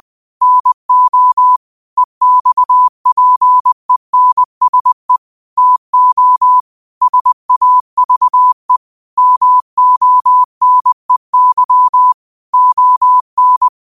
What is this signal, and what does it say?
Morse code, heard as a keyed tone, about 20 wpm, 1000 Hz. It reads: NO EXPENSE TO SAVE MONEY ON